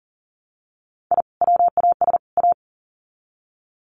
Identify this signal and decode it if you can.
Morse code, heard as a keyed tone, about 40 wpm, 700 Hz. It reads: I WAS A
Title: Sets of 3 Words 40wpm